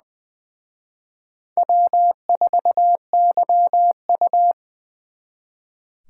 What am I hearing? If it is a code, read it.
Morse code, heard as a keyed tone, about 24 wpm, 700 Hz. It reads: W4YU
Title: Speed Racing - USA Call Signs 20wpm